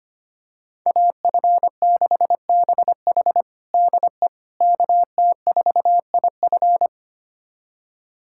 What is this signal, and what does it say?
Morse code, heard as a keyed tone, about 25 wpm, 700 Hz. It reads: AF6BH DE KT4IF
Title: USA Callsign DE Callsign - 5 Characters - 25wpm